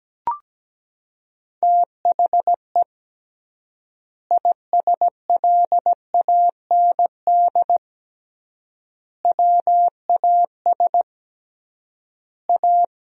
Morse code, heard as a keyed tone, about 17 wpm, 700 Hz. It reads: THE ISLAND WAS A
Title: 3x Word-Spacing - Sentences from Top 500 Words - 17wpm